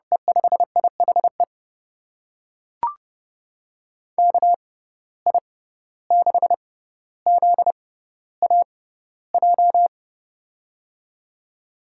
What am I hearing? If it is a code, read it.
Morse code, heard as a keyed tone, about 31 wpm, spaced as for 15 wpm, 700 Hz. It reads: E5IHE KI6ZAJ
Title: USA Call Signs - 6 Characters - 6x Character Spacing - 30wpm